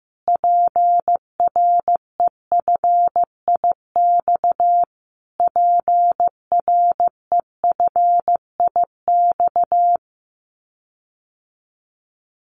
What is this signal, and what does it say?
Morse code, heard as a keyed tone, about 15 wpm, 700 Hz. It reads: PREFIX PREFIX
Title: Callsign Prefixes for Top 25 DXCC - Repeated 3x - 15wpm